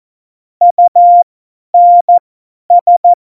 Morse code, heard as a keyed tone, about 14 wpm, spaced as for 10 wpm, 700 Hz. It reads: UNS
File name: Morsekurs / Witze / Witz 20. August 2019, 50 F70